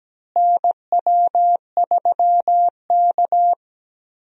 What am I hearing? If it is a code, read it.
Morse code, heard as a keyed tone, about 17 wpm, 700 Hz. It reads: NW3K
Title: USA Call Signs - 4 Characters - Repeated 2x - 17wpm